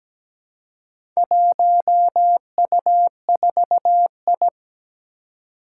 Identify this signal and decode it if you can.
Morse code, heard as a keyed tone, about 17 wpm, 700 Hz. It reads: 1U4I
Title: Four Challenging Letter-Number Combinations 17wpm